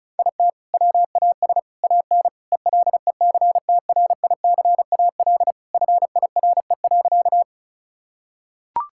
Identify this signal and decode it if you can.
Morse code, heard as a keyed tone, about 34 wpm, 700 Hz. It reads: IT WAS AN ELECTRICAL FIRE.
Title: Sentences from Top 1000 Words - Review Part 6 - 35wpm